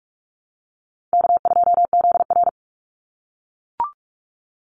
Morse code, heard as a keyed tone, about 44 wpm, 700 Hz. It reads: K2ZR
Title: Call Signs - 45wpm